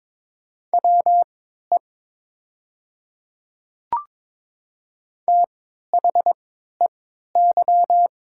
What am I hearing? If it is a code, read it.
Morse code, heard as a keyed tone, about 22 wpm, spaced as for 12 wpm, 700 Hz. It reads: WE THEY
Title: 3x Character Spacing - Top 100 Words - 22wpm